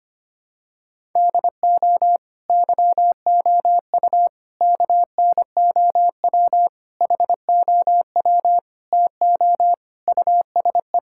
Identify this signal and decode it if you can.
Morse code, heard as a keyed tone, about 25 wpm, 700 Hz. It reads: DO YOU KNOW HOW TO USE
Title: Sentences from Top 1000 Words - Review Part 6 - 25wpm